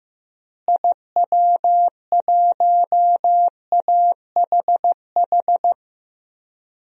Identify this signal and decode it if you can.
Morse code, heard as a keyed tone, about 15 wpm, 700 Hz. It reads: IW1AHH
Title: DXCC-100 Call Signs - 6 Characters - Repeated 2x - 15wpm